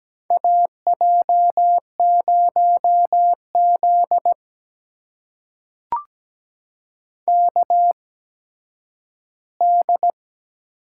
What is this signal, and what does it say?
Morse code, heard as a keyed tone, about 17 wpm, spaced as for 10 wpm, 700 Hz. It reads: AJ0Z KD
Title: USA Call Signs - 8x Character Spacing - 17wpm